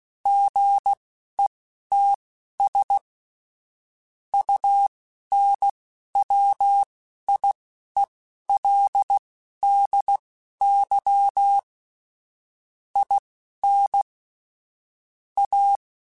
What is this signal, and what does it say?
Morse code, heard as a keyed tone, about 16 wpm, spaced as for 11 wpm, 800 Hz. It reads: GETS UNWIELDY IN A
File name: Technical Subjects / Technical Subjects-mp3s / AND Gate @10